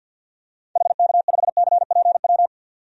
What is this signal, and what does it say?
Morse code, heard as a keyed tone, about 53 wpm, 700 Hz. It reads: HK5CPW
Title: DXCC-100 Call Signs - 6 Characters - 50wpm